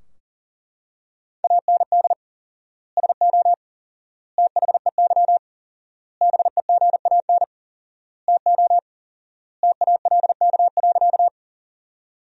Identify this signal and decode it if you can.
Morse code, heard as a keyed tone, about 37 wpm, 700 Hz. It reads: AND SO THEY BEGAN TO TALK.
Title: 4x Word-Spacing - Sentences from Top 400 Words - 40wpm